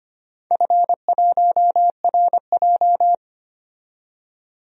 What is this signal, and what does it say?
Morse code, heard as a keyed tone, about 25 wpm, 700 Hz. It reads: F1RJ
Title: DXCC-100 Call Signs - 4 Characters - 25wpm